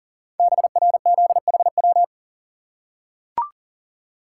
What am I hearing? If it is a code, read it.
Morse code, heard as a keyed tone, about 42 wpm, 690 Hz. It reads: BRZHW
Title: Code Groups - 5 Letters - 40wpm